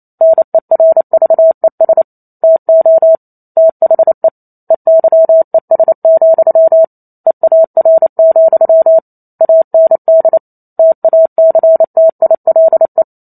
Morse code, heard as a keyed tone, about 28 wpm, 650 Hz. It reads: NERVES TO THE EYES, EAR, AND TACTILE